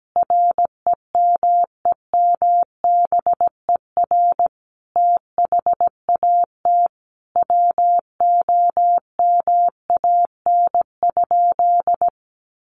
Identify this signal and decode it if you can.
Morse code, heard as a keyed tone, about 17 wpm, 700 Hz. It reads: REMEMBER THAT WOMAN?
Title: Sentences from Top 1100 Words 17wpm